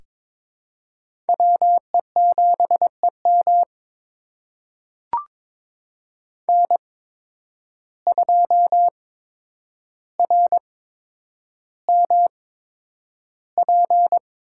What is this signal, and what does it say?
Morse code, heard as a keyed tone, about 22 wpm, spaced as for 9 wpm, 700 Hz. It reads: WE7EM N2RMP
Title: USA Call Signs - 8x Character Spacing - 22wpm